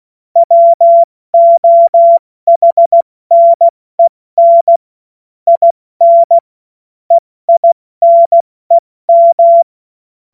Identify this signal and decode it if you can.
Morse code, heard as a keyed tone, about 16 wpm, spaced as for 14 wpm, 680 Hz. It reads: WOHNEN IN EINEM
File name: Morsekurs / Witze / Witz 14. Juni 2022 70 F80